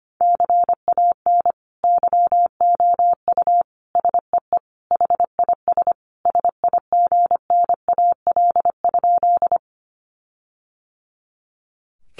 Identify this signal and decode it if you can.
Morse code, heard as a keyed tone, about 25 wpm, 700 Hz. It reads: CAN YOU SEE HIS SIGNAL?
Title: Sentences from Top 1400 Words 25wpm